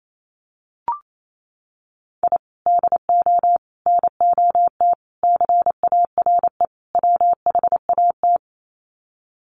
Morse code, heard as a keyed tone, about 28 wpm, 700 Hz. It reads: I DO NOT CARE WHAT
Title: Sets of 5 Words 28wpm